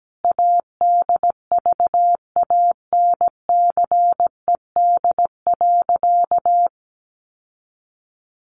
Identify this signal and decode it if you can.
Morse code, heard as a keyed tone, about 17 wpm, 700 Hz. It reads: ADVANCED.
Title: Sentences from Top 1200 Words 17wpm